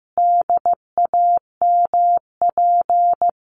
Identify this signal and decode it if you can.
Morse code, heard as a keyed tone, about 15 wpm, 700 Hz. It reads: DAMP
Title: A Sound Of Thunder 15wpm